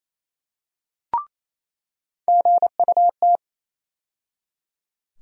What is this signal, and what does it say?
Morse code, heard as a keyed tone, about 28 wpm, 700 Hz. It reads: GUT
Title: USA Call Sign Postfixes - 3 Letters 28wpm